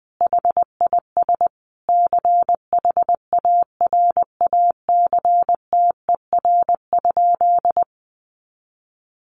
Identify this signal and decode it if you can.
Morse code, heard as a keyed tone, about 20 wpm, 700 Hz. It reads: HIS CHARACTER?
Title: Sentences from Top 1200 Words 20wpm